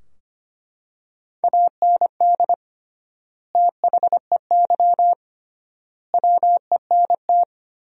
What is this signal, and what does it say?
Morse code, heard as a keyed tone, about 25 wpm, 700 Hz. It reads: AND THEY WENT
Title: 3x Word-Spacing - Sentences from Top 200 Words - 25wpm